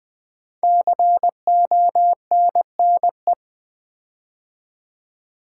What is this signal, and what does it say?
Morse code, heard as a keyed tone, about 20 wpm, 700 Hz. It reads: CONNE
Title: Build A Word - 5 Letters At A Time - 10-Letter Words - 20wpm